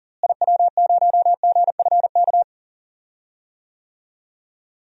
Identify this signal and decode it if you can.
Morse code, heard as a keyed tone, about 40 wpm, 700 Hz. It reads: IW0GFK
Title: DXCC-100 Call Signs - 6 Characters - Repeated 2x - 40wpm